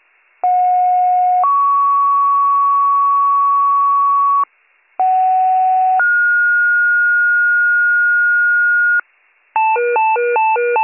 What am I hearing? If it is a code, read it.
Two-tone page, heard as a one-second tone, then a three-second tone, about 730 then 1120 Hz, about 750 then 1430 Hz, with noise. It is heard from 0.4 to 9.0 s.